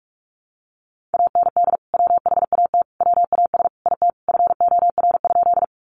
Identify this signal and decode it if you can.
Morse code, heard as a keyed tone, about 44 wpm, 700 Hz. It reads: AND WHAT WAS IT FOR?
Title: Sentences from Top 100 Words 45wpm